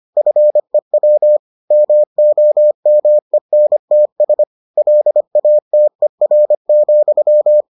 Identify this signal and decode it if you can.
Morse code, heard as a keyed tone, about 25 wpm, 600 Hz. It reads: FEW MOMENTS LATER,